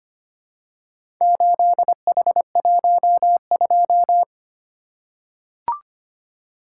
Morse code, heard as a keyed tone, about 25 wpm, 700 Hz. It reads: 8H12
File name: practice / 25wpm / Four Letter-Number Combinations - 25wpm